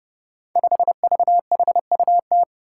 Morse code, heard as a keyed tone, about 30 wpm, 700 Hz. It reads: 5VHUT
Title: Five Challenging Letter-Number Combinations 30wpm